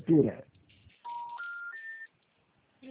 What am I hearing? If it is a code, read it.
Special information tone, with speech.
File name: out- -2002- - - .18104